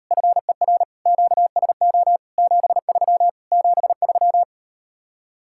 Morse code, heard as a keyed tone, about 38 wpm, 700 Hz. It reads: FER QSO 73 73